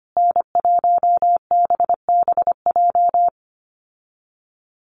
Morse code, heard as a keyed tone, about 25 wpm, 700 Hz. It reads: N1BBJ
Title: USA Call Signs Repeated 3x 25wpm